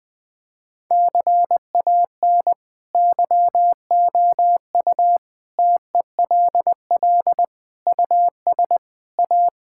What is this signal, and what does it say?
Morse code, heard as a keyed tone, about 20 wpm, 700 Hz. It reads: CAN YOU TELL US A
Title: Sentences with 8-Word Phrases - 20wpm